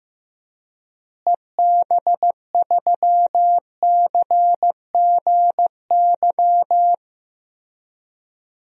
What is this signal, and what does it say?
Morse code, heard as a keyed tone, about 15 wpm, 700 Hz. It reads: EB3CGY
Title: DXCC-100 Call Signs - 6 Characters - Repeated 3x - 15wpm